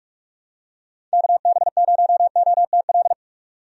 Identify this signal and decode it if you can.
Morse code, heard as a keyed tone, about 46 wpm, 700 Hz. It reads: KB0OTL
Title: USA Call Signs - 6 Characters - 45wpm